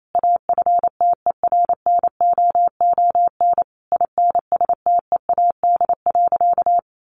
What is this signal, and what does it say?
Morse code, heard as a keyed tone, about 28 wpm, 700 Hz. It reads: AFTERNOON INSTEAD.
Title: Sentences from Top 1500 Words 28wpm